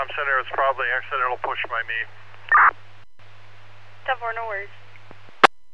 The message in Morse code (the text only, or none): none